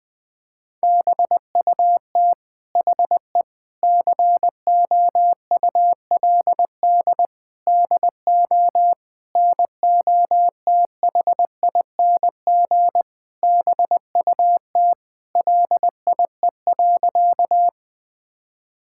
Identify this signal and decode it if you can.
Morse code, heard as a keyed tone, about 19 wpm, 700 Hz. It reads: BUT HE COULD DO NOTHING BUT LIE.
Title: Sentences from Top 1600 Words 20wpm